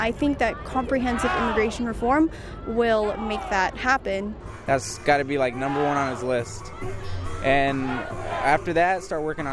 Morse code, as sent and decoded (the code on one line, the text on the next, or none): none